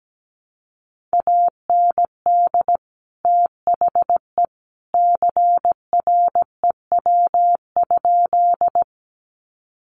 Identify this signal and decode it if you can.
Morse code, heard as a keyed tone, about 17 wpm, 700 Hz. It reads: AND THE CREW?
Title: Sentences from Top 1500 Words 17wpm